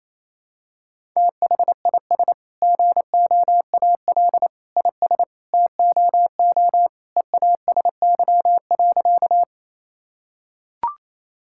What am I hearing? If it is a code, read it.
Morse code, heard as a keyed tone, about 27 wpm, 700 Hz. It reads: THIS GOAL IS TOO EASY.
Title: Sentences from Top 1000 Words - Review Part 5 - 28wpm